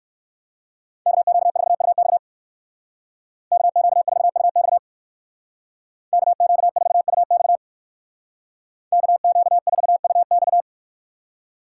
Morse code, heard as a keyed tone, about 59 wpm, 700 Hz. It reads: KQ4UX KQ4UX KQ4UX KQ4UX